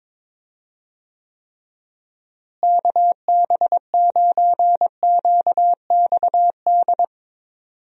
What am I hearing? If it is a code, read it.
Morse code, heard as a keyed tone, about 22 wpm, 700 Hz. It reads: KB9QXD
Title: USA Call Signs - 6 Characters - Repeated 3x - 22wpm